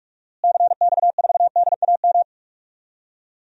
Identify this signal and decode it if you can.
Morse code, heard as a keyed tone, about 46 wpm, 700 Hz. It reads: CX4DAM
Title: DXCC-100 Call Signs - 6 Characters - Repeated 2x - 45wpm